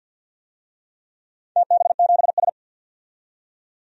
Morse code, heard as a keyed tone, about 48 wpm, 700 Hz. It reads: TB7S
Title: DXCC-100 Call Signs - 4 Characters - 50wpm